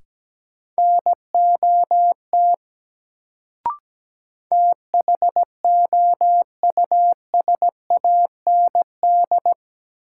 Morse code, heard as a keyed tone, about 17 wpm, 700 Hz. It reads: NOT THOUSAND
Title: Sentences from Top 600 Words - New Word Drill - 17wpm